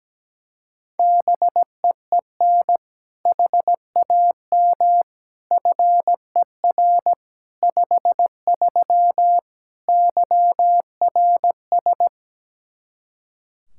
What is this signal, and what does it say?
Morse code, heard as a keyed tone, about 17 wpm, 700 Hz. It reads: BEEN HAM FER 53 YRS
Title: Practice QSO - Exchange 3 - More Info - 17wpm